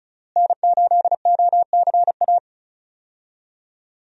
Morse code, heard as a keyed tone, about 36 wpm, 700 Hz. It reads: N8OCA
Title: USA Call Signs - 5 Characters - Repeated 3x - 35wpm